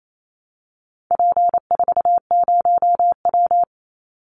Morse code, heard as a keyed tone, about 28 wpm, 700 Hz. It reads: P40W
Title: Call Signs Repeated 3x - 28wpm